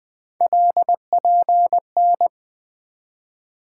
Morse code, heard as a keyed tone, about 21 wpm, 700 Hz. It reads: LPN
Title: K1USN SST QSO Practice 20wpm